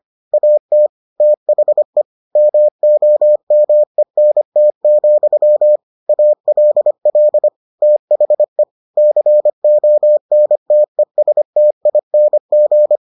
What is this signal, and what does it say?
Morse code, heard as a keyed tone, about 25 wpm, 600 Hz. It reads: AT THE MOMENT, ALL THE CONTESTING